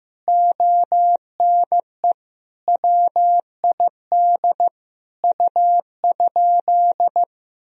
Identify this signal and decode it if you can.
Morse code, heard as a keyed tone, about 15 wpm, 700 Hz. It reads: ONE WID U?
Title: Sentences from Top 500 Words and Abbreviations 15wpm